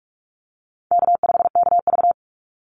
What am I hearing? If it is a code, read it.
Morse code, heard as a keyed tone, about 44 wpm, 700 Hz. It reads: K5KV
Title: Easy Call Signs Repeated 2x - 45wpm